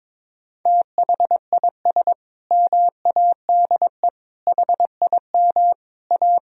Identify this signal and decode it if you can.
Morse code, heard as a keyed tone, about 22 wpm, 700 Hz. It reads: THIS MADE HIM A
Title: Sentences from Top 1000 Words - Review Part 3 - 22wpm